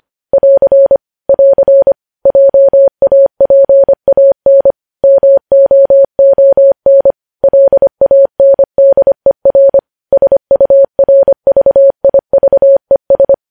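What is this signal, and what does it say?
Morse code, heard as a keyed tone, about 25 wpm, 550 Hz. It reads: <AR> <AR> JAPAN MOON LANDER SURVIVES
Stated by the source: News headlines sent at 25.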